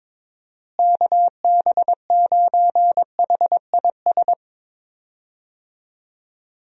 Morse code, heard as a keyed tone, about 22 wpm, 700 Hz. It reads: KB9HIS
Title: USA Call Signs - 6 Characters - Repeated 2x - 22wpm